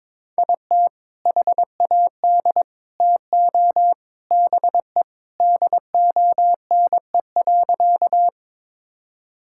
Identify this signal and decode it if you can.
Morse code, heard as a keyed tone, about 21 wpm, 700 Hz. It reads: IT HAD TO BE DONE.
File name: practice / 22wpm / Sentences with 3-Word Phrases - 22wpm